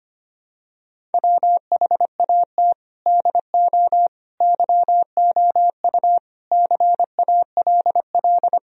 Morse code, heard as a keyed tone, about 25 wpm, 700 Hz. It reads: WHAT DO YOU CALL
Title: Sentences from Top 1000 Words - Review Part 3 - 25wpm